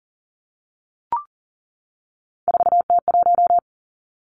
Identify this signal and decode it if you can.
Morse code, heard as a keyed tone, about 40 wpm, 700 Hz. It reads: 4T1
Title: Three Challenging Letter-Number Combinations 40wpm